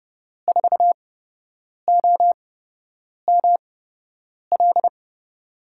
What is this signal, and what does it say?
Morse code, heard as a keyed tone, about 31 wpm, spaced as for 8 wpm, 700 Hz. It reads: 4OML
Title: USA Call Signs - 5 Characters - 8x Character Spacing - 30wpm